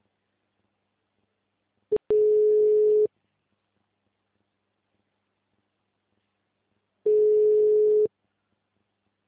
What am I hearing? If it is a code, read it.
Ringback tone (Europe, 425 Hz).
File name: out- -2017- - - .22609